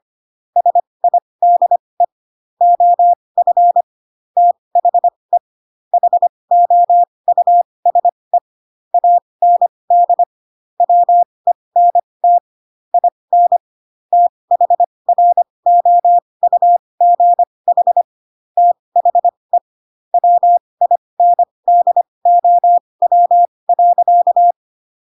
Morse code, heard as a keyed tone, about 25 wpm, spaced as for 20 wpm, 700 Hz. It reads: SIDE OF THE HOUSE AND WENT IN THROUGH THE WINDOW.
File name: Pinocchio / Pinocchio ch7 15 25 ik1hge